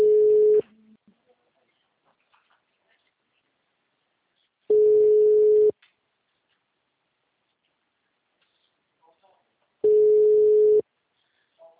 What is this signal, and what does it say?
Ringback tone (Europe, 425 Hz), with speech.